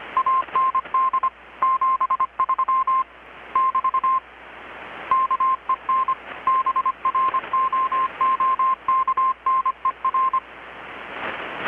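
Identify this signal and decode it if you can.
Morse code, heard as a keyed tone, about 25 wpm, 1050 Hz. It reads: AND 73 <BT> KEN BROOKNER
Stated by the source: Morse from KPH's Night of Nights XX maritime radiotelegraphy event, received in Chiapas, Mexico about 3,800 km away on a Kenwood R-600 with a grounded 10 m long-wire antenna.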